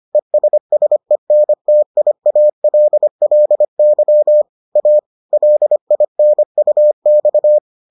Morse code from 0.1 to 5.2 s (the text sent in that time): ESSENTIALLY A